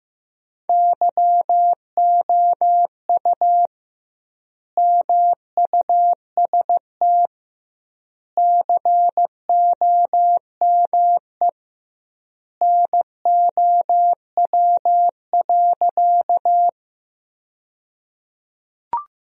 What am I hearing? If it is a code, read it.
Morse code, heard as a keyed tone, about 15 wpm, 700 Hz. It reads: YOU MUST COME NOW.
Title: 2x Word-Spacing - Sentences from Top 200 Words - 15wpm